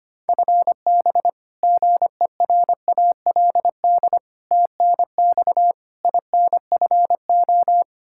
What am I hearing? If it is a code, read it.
Morse code, heard as a keyed tone, about 25 wpm, 700 Hz. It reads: FB GERALD TNX INFO
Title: Practice QSO - Exchange 3 - More Info - 25wpm